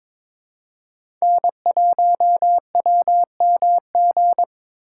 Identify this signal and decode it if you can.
Morse code, heard as a keyed tone, about 23 wpm, 700 Hz. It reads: N1WMG
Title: USA Call Signs - 4x Character Spacing - 22wpm